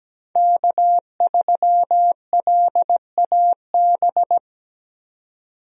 Morse code, heard as a keyed tone, about 17 wpm, 700 Hz. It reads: K3LAB
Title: USA Call Signs - 5 Characters - 17wpm